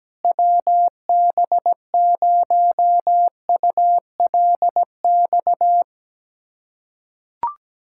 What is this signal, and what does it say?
Morse code, heard as a keyed tone, about 17 wpm, 700 Hz. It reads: WB0ULX
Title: K1USN SST QSO Practice - 17wpm